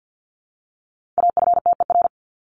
Morse code, heard as a keyed tone, about 50 wpm, 700 Hz. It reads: AFTER
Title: Top 200 Words 50wpm